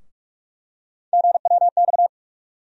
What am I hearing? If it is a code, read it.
Morse code, heard as a keyed tone, about 41 wpm, 700 Hz. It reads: GWX
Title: USA Call Sign Postfixes - 3 Letters 45wpm